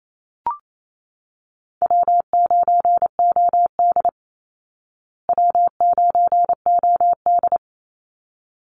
Morse code, heard as a keyed tone, about 28 wpm, 700 Hz. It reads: W9OD W9OD
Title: Easy Call Signs Repeated 2x - 28wpm